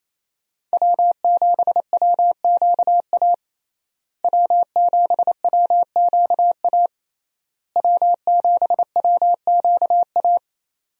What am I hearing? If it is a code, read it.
Morse code, heard as a keyed tone, about 28 wpm, 700 Hz. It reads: W7WQA W7WQA W7WQA
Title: USA Call Signs - 5 Characters - Repeated 3x - 28wpm